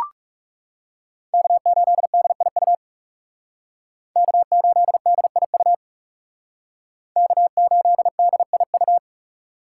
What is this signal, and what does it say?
Morse code, heard as a keyed tone, about 35 wpm, 700 Hz. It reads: K8DIU K8DIU K8DIU
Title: Speed Racing - USA 1x3 Call Signs 30wpm